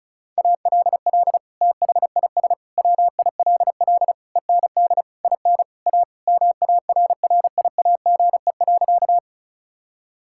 Morse code, heard as a keyed tone, about 34 wpm, 700 Hz. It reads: ALL THIS WILL END IN A MARRIAGE.
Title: Sentences from Top 1000 Words - Review Part 5 - 35wpm